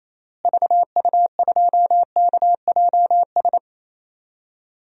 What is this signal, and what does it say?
Morse code, heard as a keyed tone, about 28 wpm, 700 Hz. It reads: VU2KJS
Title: DXCC-100 Call Signs - 6 Characters - Repeated 3x - 28wpm